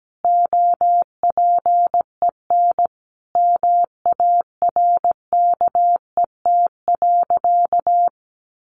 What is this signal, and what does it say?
Morse code, heard as a keyed tone, about 17 wpm, 700 Hz. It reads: OPEN MARKET.
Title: Sentences from Top 1400 Words 17wpm